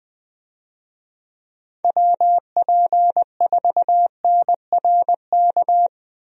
Morse code, heard as a keyed tone, about 20 wpm, 700 Hz. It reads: WP4NRK
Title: USA Call Signs - 6 Characters - Repeated 3x - 20wpm